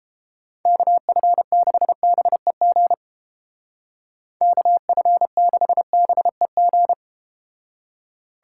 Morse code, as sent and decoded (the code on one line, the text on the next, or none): -.- ..-. -.... -... --. / -.- ..-. -.... -... --.
KF6BEG KF6BEG